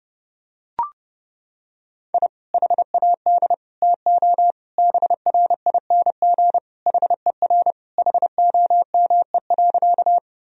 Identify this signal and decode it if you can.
Morse code, heard as a keyed tone, about 30 wpm, 700 Hz. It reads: I HAD TO BRING HER HOME.
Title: Sentences from Top 1000 Words - Review Part 3 - 30wpm